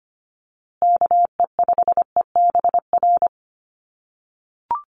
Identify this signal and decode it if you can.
Morse code, heard as a keyed tone, about 25 wpm, 700 Hz. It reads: KE5EBR
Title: USA Call Signs Repeated 2x 25wpm